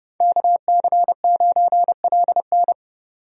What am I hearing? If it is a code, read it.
Morse code, heard as a keyed tone, about 30 wpm, 700 Hz. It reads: KC9LN
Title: USA Call Signs - 5 Characters - 30wpm